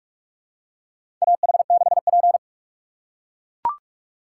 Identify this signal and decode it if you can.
Morse code, heard as a keyed tone, about 42 wpm, 700 Hz. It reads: AH6P
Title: USA Call Signs - 2x Character Spacing - 45wpm